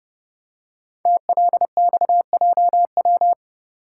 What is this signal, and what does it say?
Morse code, heard as a keyed tone, about 30 wpm, 690 Hz. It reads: TLXJW
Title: Code Groups - 5 Letters - 30wpm